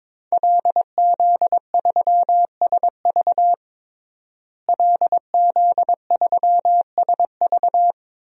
Morse code, heard as a keyed tone, about 22 wpm, 700 Hz. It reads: LZ3SV LZ3SV
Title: DXCC-100 Call Signs - 5 Characters - Repeated 2x - 22wpm